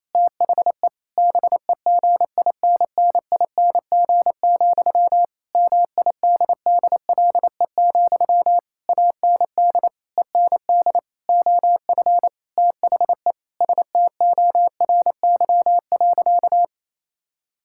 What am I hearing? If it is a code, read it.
Morse code, heard as a keyed tone, about 28 wpm, 700 Hz. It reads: THE BEGINNING, MIDDLE, AND END OF THE STORY.
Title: Sentences from Top 1000 Words - Review Part 6 - 28wpm